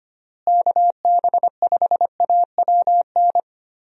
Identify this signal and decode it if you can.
Morse code, heard as a keyed tone, about 25 wpm, 700 Hz. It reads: KB5AWN
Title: USA Call Signs - 6 Characters - Repeated 3x - 25wpm